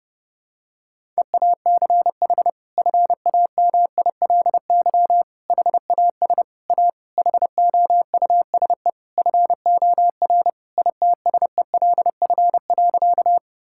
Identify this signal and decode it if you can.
Morse code, heard as a keyed tone, about 30 wpm, 700 Hz. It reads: EACH FAMILY HAS A HOUSE FOR ITSELF.
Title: Sentences from Top 1000 Words - Review Part 5 - 30wpm